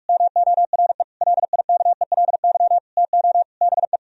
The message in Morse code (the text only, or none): MORE LIKELY TO BE